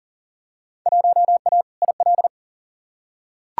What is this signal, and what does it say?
Morse code, heard as a keyed tone, about 40 wpm, 700 Hz. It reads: 1A IL
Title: Field Day Exchanges - 40wpm